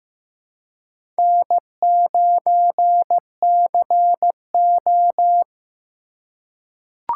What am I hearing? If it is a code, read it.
Morse code, heard as a keyed tone, about 15 wpm, 700 Hz. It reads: N9CO
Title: USA Call Signs - 4 Characters - 4x Character Spacing - 15wpm